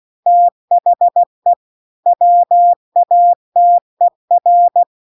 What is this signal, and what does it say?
Morse code, heard as a keyed tone, about 16 wpm, 700 Hz. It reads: THE WATER